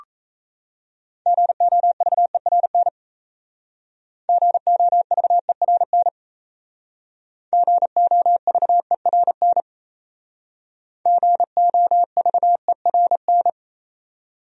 Morse code, heard as a keyed tone, about 36 wpm, 700 Hz. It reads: GOVERN GOVERN GOVERN GOVERN